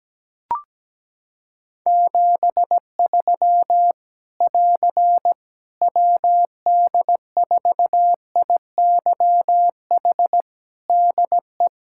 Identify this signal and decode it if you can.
Morse code, heard as a keyed tone, about 17 wpm, 700 Hz. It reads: 73 <AR> WD4IYH DE
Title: Practice QSO - Exchange 4 - Ending - 17wpm